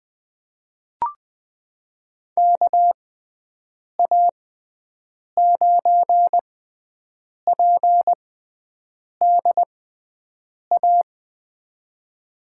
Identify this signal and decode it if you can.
Morse code, heard as a keyed tone, about 20 wpm, spaced as for 7 wpm, 700 Hz. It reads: KA9PDA
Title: USA Call Signs - 6 Characters - 6x Character Spacing - 20wpm